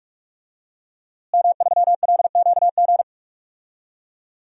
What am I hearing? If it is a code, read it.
Morse code, heard as a keyed tone, about 46 wpm, 700 Hz. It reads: M3LQG
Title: DXCC-100 Call Signs - 5 Characters - Repeated 2x - 45wpm